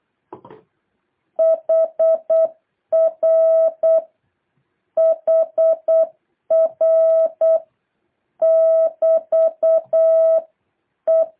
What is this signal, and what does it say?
Morse code, heard as a keyed tone, about 8 wpm, 650 Hz. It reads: HR HR <BT> E
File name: jh8cbh / 2023cw